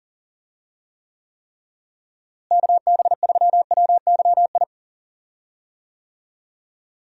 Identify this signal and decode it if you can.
Morse code, heard as a keyed tone, about 40 wpm, 700 Hz. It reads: KB3WYI
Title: USA Call Signs - 6 Characters - Repeated 3x - 40wpm